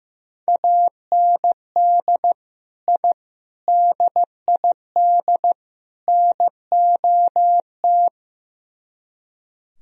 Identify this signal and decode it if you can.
Morse code, heard as a keyed tone, about 17 wpm, 700 Hz. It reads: AND I DID NOT
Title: Speed-Racing - Sets of 4 Words 15wpm